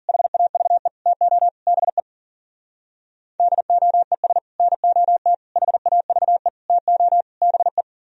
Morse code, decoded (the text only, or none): HAVE TO BE DOES NOT HAVE TO BE